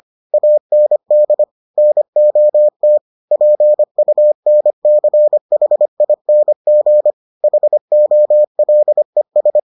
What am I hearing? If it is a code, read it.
Morse code, heard as a keyed tone, about 25 wpm, 600 Hz. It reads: AND NOT PUNCHING HOLES